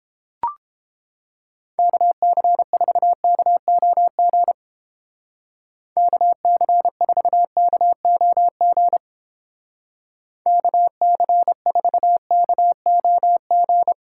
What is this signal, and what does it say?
Morse code, heard as a keyed tone, about 29 wpm, 700 Hz. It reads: KC4KOG KC4KOG KC4KOG